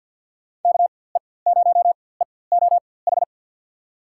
Morse code, heard as a keyed tone, about 48 wpm, spaced as for 23 wpm, 700 Hz. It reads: KE0EOH
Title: USA Call Signs - 4x Character Spacing - 50wpm